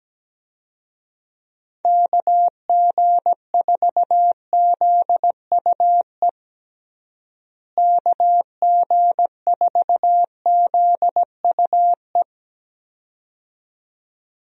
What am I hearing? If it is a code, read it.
Morse code, heard as a keyed tone, about 17 wpm, 700 Hz. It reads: KG4ZUE KG4ZUE
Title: USA Call Signs - 6 Characters - Repeated 2x - 17wpm